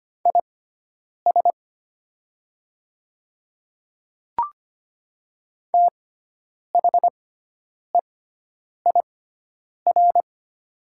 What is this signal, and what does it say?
Morse code, heard as a keyed tone, about 25 wpm, spaced as for 9 wpm, 700 Hz. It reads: IS THEIR
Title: 6x Character Spacing - Top 300 Words - 25wpm